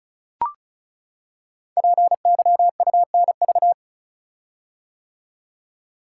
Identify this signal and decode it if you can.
Morse code, heard as a keyed tone, about 36 wpm, 700 Hz. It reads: PYUNV
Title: Code Groups - 5 Letters - 35wpm